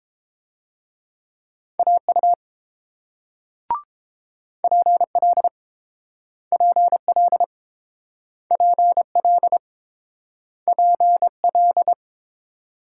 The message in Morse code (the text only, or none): AU PL PL PL PL